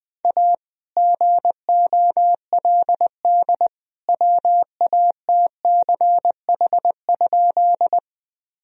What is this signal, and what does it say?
Morse code, heard as a keyed tone, about 20 wpm, 700 Hz. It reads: A GOLD WATCH?
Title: Sentences from Top 1000 Words - Review Part 3 - 20wpm